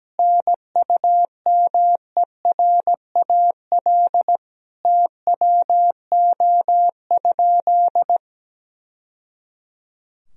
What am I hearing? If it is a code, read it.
Morse code, heard as a keyed tone, about 17 wpm, 700 Hz. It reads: NUMERAL TWO?
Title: Sentences from Top 1000 Words - Review Part 2 - 17wpm